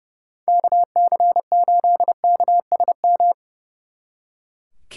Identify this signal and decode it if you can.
Morse code, heard as a keyed tone, about 36 wpm, 700 Hz. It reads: KC8KSM
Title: Speed Racing - USA Call Signs 30wpm